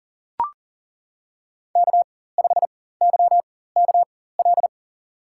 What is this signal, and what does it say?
Morse code, heard as a keyed tone, about 40 wpm, spaced as for 19 wpm, 700 Hz. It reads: K5YKL
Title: USA Call Signs - 5 Characters - 4x Character Spacing - 40wpm